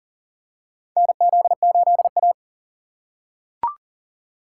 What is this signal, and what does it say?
Morse code, heard as a keyed tone, about 40 wpm, 700 Hz. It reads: NZ8A
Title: USA Call Signs - 4 Characters - 3x Character Spacing - 40wpm